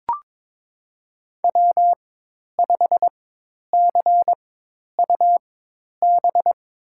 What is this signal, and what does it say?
Morse code, heard as a keyed tone, about 22 wpm, spaced as for 10 wpm, 700 Hz. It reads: W5CUB
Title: USA Call Signs - 4x Character Spacing - 22wpm